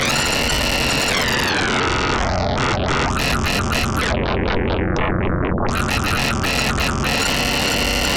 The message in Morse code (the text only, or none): none